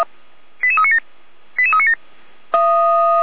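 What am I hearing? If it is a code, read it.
Five-tone selective call (ZVEI), heard as quick runs of five tones, tones of about 70 ms. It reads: 88397 88397
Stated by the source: ZVEI five-tone selective call on 455, received by SDR; Feuerwehr Orlamünde fire brigade.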